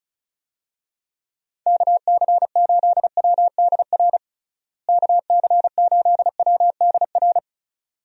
Morse code, heard as a keyed tone, about 36 wpm, 700 Hz. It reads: KC8WDR KC8WDR
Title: USA Call Signs - 6 Characters - Repeated 2x - 35wpm